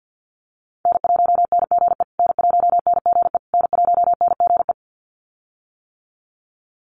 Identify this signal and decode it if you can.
Morse code, heard as a keyed tone, about 50 wpm, 700 Hz. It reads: N1NGE N1NGE N1NGE
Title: USA Call Signs Repeated 3x 50wpm